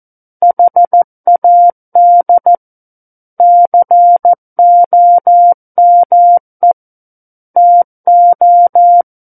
Morse code, heard as a keyed tone, about 14 wpm, 700 Hz. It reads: HAD COME TO
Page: War of the Worlds - 05-Chapter 5 - 14 WPM